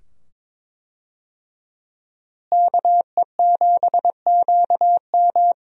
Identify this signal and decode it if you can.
Morse code, heard as a keyed tone, about 22 wpm, 700 Hz. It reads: KE7QM
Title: USA Call Signs - 5 Characters - Repeated 2x - 22wpm